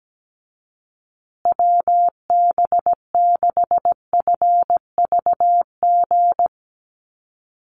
Morse code, heard as a keyed tone, about 17 wpm, 700 Hz. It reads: WB6FVG
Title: USA Call Signs Repeated 3x 17wpm